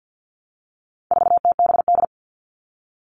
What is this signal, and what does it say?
Morse code, heard as a keyed tone, about 50 wpm, 700 Hz. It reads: 4TBD
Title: All Cut Numbers - 4-Digit 50wpm